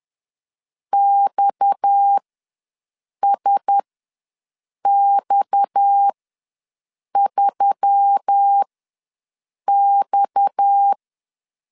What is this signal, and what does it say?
Morse code, heard as a keyed tone, about 10 wpm, spaced as for 6 wpm, 800 Hz. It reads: XSX3X
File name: LEZIONI / 6 PKX0 / 6 PKX ws 10wpm 3706